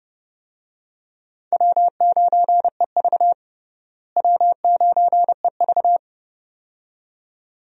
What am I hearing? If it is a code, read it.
Morse code, heard as a keyed tone, about 32 wpm, 700 Hz. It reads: W9EV W9EV
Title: USA Call Signs - 4 Characters - Repeated 2x - 30wpm